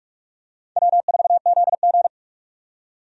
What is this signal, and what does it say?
Morse code, heard as a keyed tone, about 46 wpm, 700 Hz. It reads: W4ZG
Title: USA Call Signs - 4 Characters - 45wpm